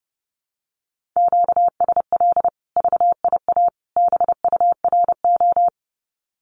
Morse code, heard as a keyed tone, about 30 wpm, 700 Hz. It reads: QSL VIA BURO
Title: Common QSO Phrases 30wpm